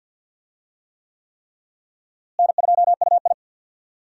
Morse code, heard as a keyed tone, about 51 wpm, 700 Hz. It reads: N2UI
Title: USA Call Signs - 4 Characters - Repeated 3x - 50wpm